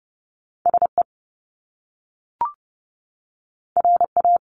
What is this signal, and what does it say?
Morse code, heard as a keyed tone, about 30 wpm, 700 Hz. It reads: SE RA
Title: Top Two Letter Combinations in Words 30wpm